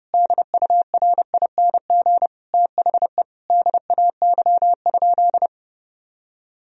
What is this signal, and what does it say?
Morse code, heard as a keyed tone, about 30 wpm, 700 Hz. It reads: DURING THE DAY?
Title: Sentences from Top 1000 Words - Review Part 2 - 30wpm